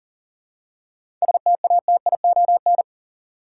Morse code, heard as a keyed tone, about 37 wpm, 700 Hz. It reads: STATION
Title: Seven Letter Words - 40wpm